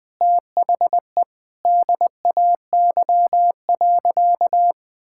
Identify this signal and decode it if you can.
Morse code, heard as a keyed tone, about 21 wpm, 700 Hz. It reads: THE DAY.
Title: Sentences with 2-Word Phrases - 20wpm